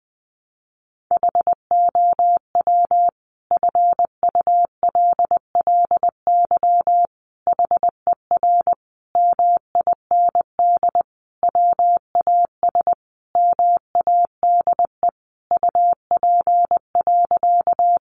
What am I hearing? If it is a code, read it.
Morse code, heard as a keyed tone, about 20 wpm, 700 Hz. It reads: HOW FULLY HER MIND WAS MADE UP.
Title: Sentences from Top 1300 Words 20wpm